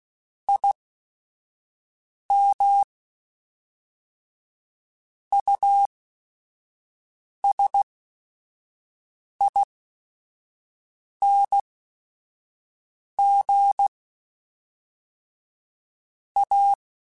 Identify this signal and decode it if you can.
Morse code, heard as a keyed tone, about 16 wpm, spaced as for 5 wpm, 800 Hz. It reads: IM USING A
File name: QSOs / QSOs-mp3s / WD0FBT 1@5